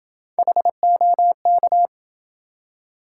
Morse code, heard as a keyed tone, about 27 wpm, 700 Hz. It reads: HOK